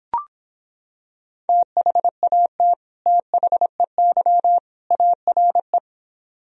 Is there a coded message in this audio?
Yes: Morse code from 1.5 to 5.8 s, heard as a keyed tone, about 26 wpm, 700 Hz. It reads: THAT THEY ARE